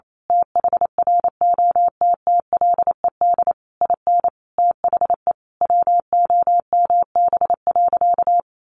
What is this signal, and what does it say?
Morse code, heard as a keyed tone, about 28 wpm, 700 Hz. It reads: THROTTLED IN THE WOMB.
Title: A Sound Of Thunder 28wpm